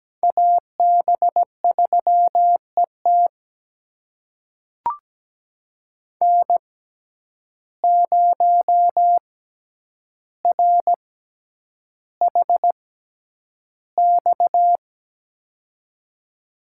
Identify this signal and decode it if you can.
Morse code, heard as a keyed tone, about 17 wpm, spaced as for 9 wpm, 700 Hz. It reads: AB3ET N0RHX
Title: USA Call Signs - 5 Characters - 6x Character Spacing - 17wpm